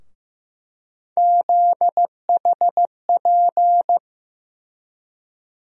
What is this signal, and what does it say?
Morse code, heard as a keyed tone, about 15 wpm, 700 Hz. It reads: ZHP
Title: USA Call Sign Postfixes - 3 Letters 15wpm